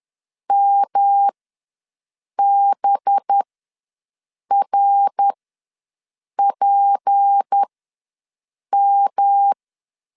Morse code, heard as a keyed tone, about 10 wpm, spaced as for 6 wpm, 800 Hz. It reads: MBRPM
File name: LEZIONI / 7 GF9Z / 7 GF ws 10wpm 3918